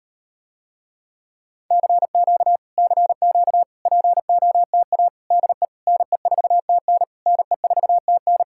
Morse code, heard as a keyed tone, about 38 wpm, 700 Hz. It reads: CQ CQ POTA DE NE4TN NE4TN